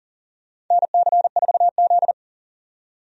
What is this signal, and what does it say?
Morse code, heard as a keyed tone, about 40 wpm, 700 Hz. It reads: NC4Z
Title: USA Call Signs - 4 Characters - 40wpm